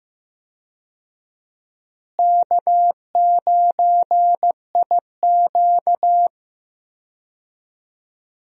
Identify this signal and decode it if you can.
Morse code, heard as a keyed tone, about 15 wpm, 700 Hz. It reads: K9IQ
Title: USA Call Signs - 4 Characters - Repeated 2x - 15wpm